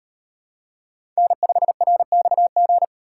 Morse code, heard as a keyed tone, about 38 wpm, 700 Hz. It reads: N5RXG